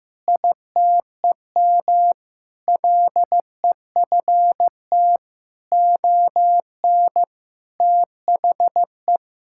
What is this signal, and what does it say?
Morse code, heard as a keyed tone, about 15 wpm, 700 Hz. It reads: ITEM LEFT ON THE
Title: Sentences from Top 1000 Words - Review Part 6 - 15wpm